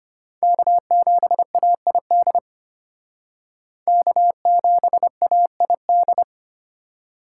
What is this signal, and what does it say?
Morse code, heard as a keyed tone, about 33 wpm, 700 Hz. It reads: K7AID K7AID